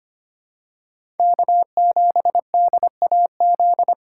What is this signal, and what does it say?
Morse code, heard as a keyed tone, about 25 wpm, 700 Hz. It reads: K7DAZ
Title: USA Call Signs - 5 Characters - 4x Character Spacing - 25wpm